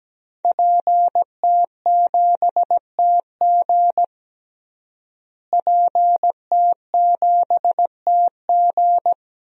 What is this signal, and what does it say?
Morse code, heard as a keyed tone, about 17 wpm, 700 Hz. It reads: PT7TG PT7TG
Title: DXCC-100 Call Signs - 5 Characters - Repeated 2x - 17wpm